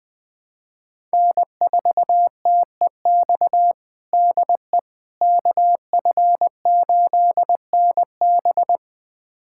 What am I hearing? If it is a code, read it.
Morse code, heard as a keyed tone, about 20 wpm, 700 Hz. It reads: N4TEX DE KF8NB
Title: USA Callsign DE Callsign - 5 Characters - 20wpm